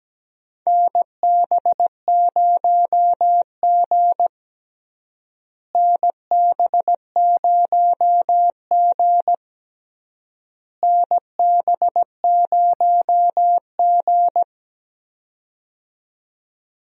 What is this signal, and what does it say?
Morse code, heard as a keyed tone, about 17 wpm, 700 Hz. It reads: NB0G NB0G NB0G
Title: USA Call Signs - 4 Characters - Repeated 3x - 17wpm